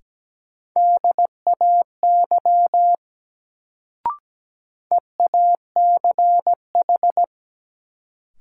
Morse code, heard as a keyed tone, about 17 wpm, 700 Hz. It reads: DAY EACH
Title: Sentences from Top 200 Words - New Word Drill - 17wpm